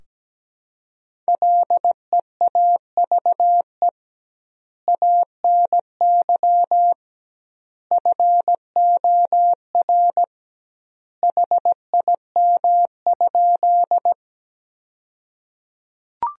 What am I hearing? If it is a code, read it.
Morse code, heard as a keyed tone, about 17 wpm, 700 Hz. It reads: LEAVE ANY FOR HIM?
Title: 2x Word-Spacing - Sentences from Top 400 Words - 17wpm